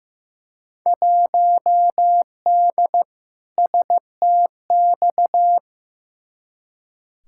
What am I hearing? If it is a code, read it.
Morse code, heard as a keyed tone, about 15 wpm, 700 Hz. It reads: 1D STX
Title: Field Day Exchanges - 15wpm